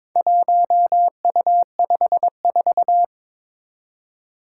Morse code, heard as a keyed tone, about 23 wpm, 700 Hz. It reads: 1U54
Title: Four Challenging Letter-Number Combinations 22wpm